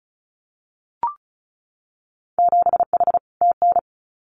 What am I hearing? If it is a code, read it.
Morse code, heard as a keyed tone, about 36 wpm, 700 Hz. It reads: 7H TN
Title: Winter Field Day Exchanges Encore 35wpm